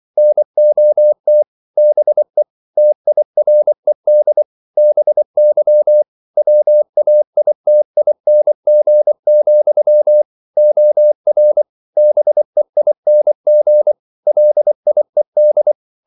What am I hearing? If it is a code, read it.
Morse code, heard as a keyed tone, about 24 wpm, 600 Hz. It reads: NOT BE TIRED BY WAITING, OR BEING LIED
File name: CW / Practice / If24